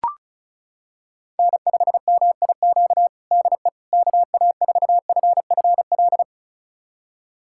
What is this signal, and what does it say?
Morse code, heard as a keyed tone, about 34 wpm, 700 Hz. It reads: N5MIQ DE KA4FFL
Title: USA Callsign DE Callsign 35wpm